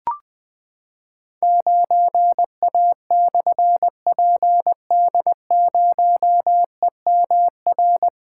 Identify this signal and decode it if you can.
Morse code, heard as a keyed tone, about 20 wpm, 700 Hz. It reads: 9A/PD0EMR
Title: Difficult Call Signs 20wpm